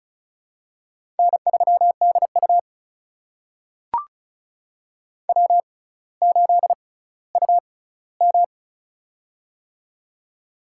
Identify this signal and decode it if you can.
Morse code, heard as a keyed tone, about 34 wpm, spaced as for 18 wpm, 700 Hz. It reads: N3DU W8UM
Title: USA Call Signs - 4 Characters - 6x Character Spacing - 35wpm